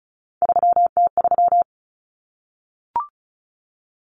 Morse code, heard as a keyed tone, about 34 wpm, 700 Hz. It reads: 3T3
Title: Common Cut Numbers - 3-Digit 35wpm